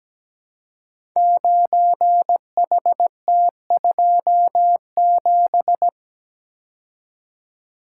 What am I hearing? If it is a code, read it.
Morse code, heard as a keyed tone, about 17 wpm, 700 Hz. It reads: 9HT27
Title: Code Groups - 5 Letter:Numbers - 17wpm